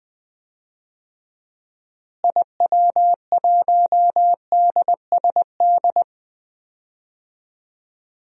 Morse code, heard as a keyed tone, about 20 wpm, 700 Hz. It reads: IW1DSD
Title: DXCC-100 Call Signs - 6 Characters - Repeated 2x - 20wpm